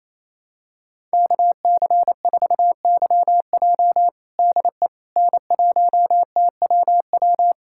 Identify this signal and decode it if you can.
Morse code, heard as a keyed tone, about 28 wpm, 700 Hz. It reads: KC4YJ DE N1TWW
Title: USA Callsign DE Callsign - 5 Characters - 28wpm